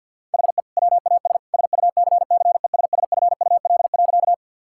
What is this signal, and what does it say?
Morse code, heard as a keyed tone, about 50 wpm, 700 Hz. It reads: HE WAS SUCCESSFUL.
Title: Sentences from Top 1000 Words - Review Part 6 - 50wpm